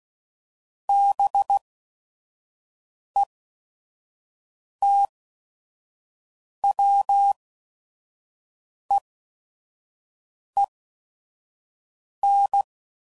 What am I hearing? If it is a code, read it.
Morse code, heard as a keyed tone, about 16 wpm, spaced as for 5 wpm, 800 Hz. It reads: BETWEEN
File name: CW Code Groups @5WPM / Weather Sets / Weather 4 24